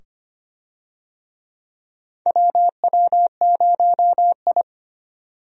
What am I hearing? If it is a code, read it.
Morse code, heard as a keyed tone, about 25 wpm, 700 Hz. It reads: WW0I
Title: USA Call Signs - 4 Characters - 25wpm